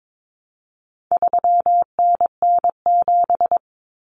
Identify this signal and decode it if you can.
Morse code, heard as a keyed tone, about 22 wpm, 700 Hz. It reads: 3NN7
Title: Common Cut Numbers - 4-Digit 22wpm